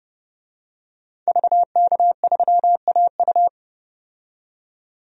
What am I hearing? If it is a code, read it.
Morse code, heard as a keyed tone, about 29 wpm, 700 Hz. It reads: VK3AU
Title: DXCC-100 Call Signs - 5 Characters - 30wpm